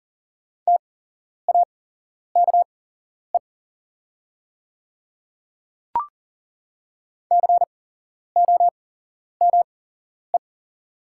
Morse code, heard as a keyed tone, about 41 wpm, spaced as for 11 wpm, 700 Hz. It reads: TAKE COME
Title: 8x Character Spacing - Top 200 Words - 40wpm